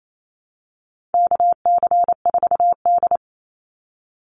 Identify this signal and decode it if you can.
Morse code, heard as a keyed tone, about 28 wpm, 700 Hz. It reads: KC4D
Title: Call Signs Repeated 3x - 28wpm